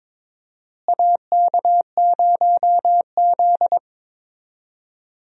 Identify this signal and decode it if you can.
Morse code, heard as a keyed tone, about 22 wpm, 700 Hz. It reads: AK0Z
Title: USA Call Signs - 4 Characters - Repeated 2x - 22wpm